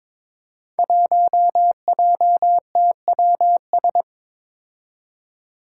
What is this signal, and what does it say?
Morse code, heard as a keyed tone, about 23 wpm, 700 Hz. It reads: 1JTWS
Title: Five Challenging Letter-Number Combinations 22wpm